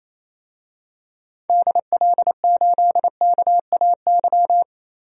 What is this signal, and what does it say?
Morse code, heard as a keyed tone, about 28 wpm, 700 Hz. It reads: DL8KAY
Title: DXCC-100 Call Signs - 6 Characters - 28wpm